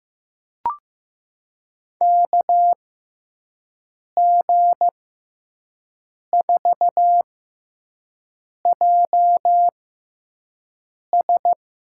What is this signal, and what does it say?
Morse code, heard as a keyed tone, about 15 wpm, spaced as for 5 wpm, 700 Hz. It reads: KG4JS
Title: USA Call Signs - 6x Character Spacing - 15wpm